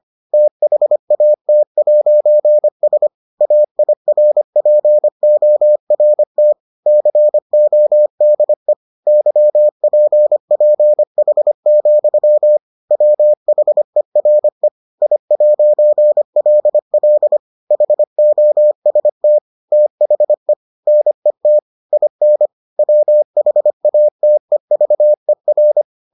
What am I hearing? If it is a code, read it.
Morse code, heard as a keyed tone, about 25 wpm, 600 Hz. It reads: THAT'S AIRPORT CODE YPPH, WHERE I'LL HOST THE NET IN WHATEVER